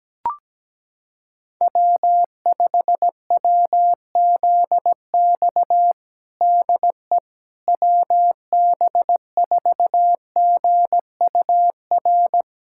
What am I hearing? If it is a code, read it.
Morse code, heard as a keyed tone, about 17 wpm, 700 Hz. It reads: W5WZX DE WB4GUR
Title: USA Callsign DE Callsign 17wpm